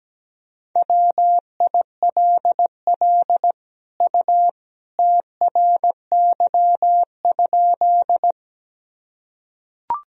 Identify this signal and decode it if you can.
Morse code, heard as a keyed tone, about 17 wpm, 700 Hz. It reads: WILL U TRY?
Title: Sentences from Top 200 Words and Abbreviations 17wpm